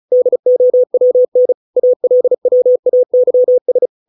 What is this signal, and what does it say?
Morse code, heard as a keyed tone, about 34 wpm, 500 Hz. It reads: DOWN ALWAYS